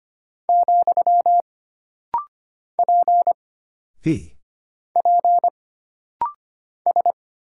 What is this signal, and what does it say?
Morse code, heard as a keyed tone, about 25 wpm, 700 Hz. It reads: , P P S
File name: Single Character (Letter:Number:Punctuation:Prosign) - Rapid-Fire - 25wpm